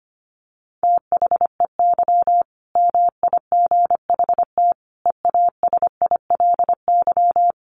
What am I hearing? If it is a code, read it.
Morse code, heard as a keyed tone, about 25 wpm, 700 Hz. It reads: THEY MIGHT EASILY
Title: Sentences from Top 1100 Words 25wpm